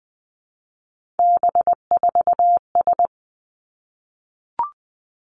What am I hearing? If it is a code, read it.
Morse code, heard as a keyed tone, about 20 wpm, 700 Hz. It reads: B4S
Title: Three Challenging Letter-Number Combinations 20wpm